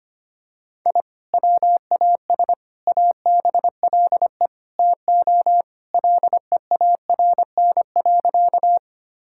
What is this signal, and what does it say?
Morse code, heard as a keyed tone, about 25 wpm, 700 Hz. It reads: I WAS ABLE TO LEARN.
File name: practice / 25wpm / Sentences with 4-Word Phrases - 25wpm